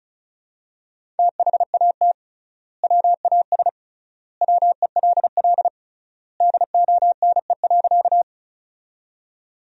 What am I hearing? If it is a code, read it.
Morse code, heard as a keyed tone, about 34 wpm, 700 Hz. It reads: THAT WAS WELL DONE.
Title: 3x Word-Spacing - Sentences from Top 500 Words - 35wpm